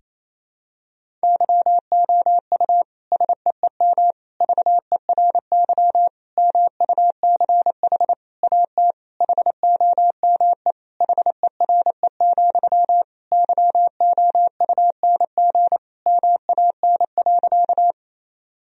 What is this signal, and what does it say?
Morse code, heard as a keyed tone, about 28 wpm, 700 Hz. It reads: YOU SEEM VERY MUCH AT HOME HERE, YOUNG MAN.
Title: Sentences from Top 1000 Words - Review Part 1 - 28wpm